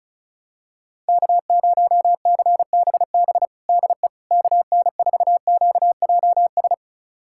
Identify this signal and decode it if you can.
Morse code, heard as a keyed tone, about 36 wpm, 700 Hz. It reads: K0CBB DE KN4QJS
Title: USA Callsign DE Callsign 35wpm